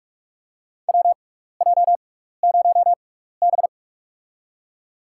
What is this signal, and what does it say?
Morse code, heard as a keyed tone, about 43 wpm, spaced as for 15 wpm, 700 Hz. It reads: WJ0B
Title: USA Call Signs - 4 Characters - 6x Character Spacing - 45wpm